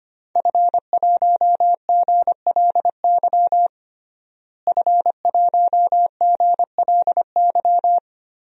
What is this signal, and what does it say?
Morse code, heard as a keyed tone, about 25 wpm, 700 Hz. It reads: F1GLY F1GLY
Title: DXCC-100 Call Signs - 5 Characters - Repeated 2x - 25wpm